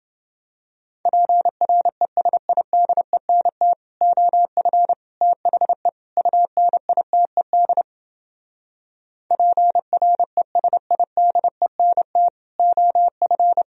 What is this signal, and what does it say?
Morse code, heard as a keyed tone, about 29 wpm, 700 Hz. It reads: PRESIDENT OF THE UNITED PRESIDENT OF